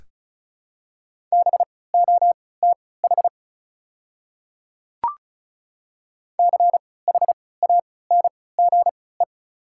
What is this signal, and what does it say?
Morse code, heard as a keyed tone, about 36 wpm, spaced as for 20 wpm, 700 Hz. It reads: BOTH CHANGE
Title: 3x Character Spacing - Top 300 Words - 35wpm